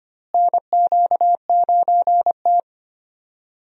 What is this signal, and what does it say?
Morse code, heard as a keyed tone, about 25 wpm, 700 Hz. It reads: NQ9T
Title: USA Call Signs - 4 Characters - Repeated 2x - 25wpm